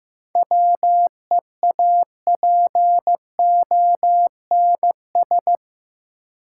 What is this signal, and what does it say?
Morse code, heard as a keyed tone, about 15 wpm, 700 Hz. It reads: WEAPONS
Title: Sets of 5 Words Encore - New Word Drill - 15wpm